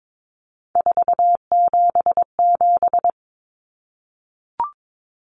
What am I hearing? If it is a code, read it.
Morse code, heard as a keyed tone, about 22 wpm, 700 Hz. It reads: 477
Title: Common Cut Numbers - 3-Digit 22wpm